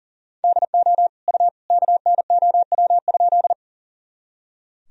Morse code, heard as a keyed tone, about 40 wpm, 700 Hz. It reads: DO U KNOW?
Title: Sentences from Top 100 Words and Abbreviations 40wpm